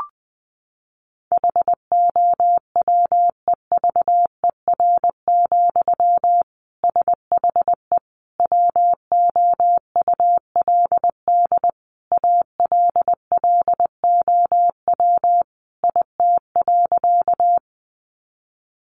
Morse code, heard as a keyed tone, about 20 wpm, 700 Hz. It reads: HOWEVER, SHE WOULD ALLOW IT.
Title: Sentences from Top 700 Words 20wpm